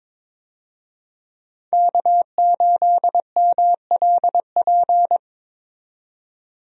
Morse code, heard as a keyed tone, about 21 wpm, 700 Hz. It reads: K8MLP
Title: USA Call Signs - 5 Characters - 22wpm